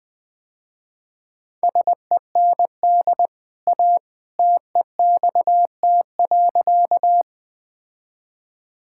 Morse code, heard as a keyed tone, about 20 wpm, 700 Hz. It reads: SEND A TEXT.
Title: Easy Sentences with Hints - 20wpm